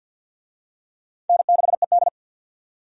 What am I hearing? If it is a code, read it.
Morse code, heard as a keyed tone, about 53 wpm, 700 Hz. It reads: N6ED
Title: USA Call Signs - 4 Characters - 50wpm